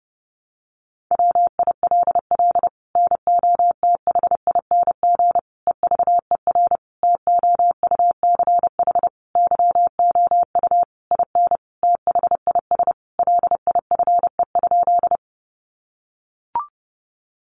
Morse code, heard as a keyed tone, about 30 wpm, 700 Hz. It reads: WILL NOTHING EVER TOUCH YOU IN THIS LIFE?
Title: Sentences from Top 1200 Words 30wpm